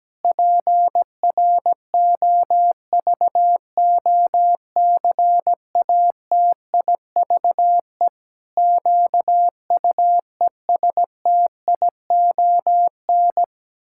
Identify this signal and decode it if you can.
Morse code, heard as a keyed tone, about 17 wpm, 700 Hz. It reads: PROVOCATIVE QUESTION
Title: Pangram Challenge v2 - 17wpm